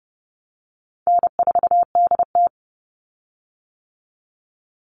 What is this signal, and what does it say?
Morse code, heard as a keyed tone, about 30 wpm, 700 Hz. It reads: N4DT
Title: Easy Call Signs - 30wpm